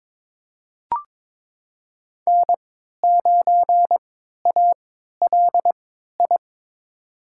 Morse code, heard as a keyed tone, about 22 wpm, spaced as for 12 wpm, 700 Hz. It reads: N9ALI
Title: USA Call Signs - 5 Characters - 3x Character Spacing - 22wpm